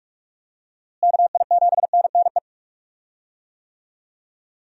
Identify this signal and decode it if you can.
Morse code, heard as a keyed tone, about 43 wpm, 700 Hz. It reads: KI7NNE
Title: USA Call Signs - 6 Characters - Repeated 2x - 45wpm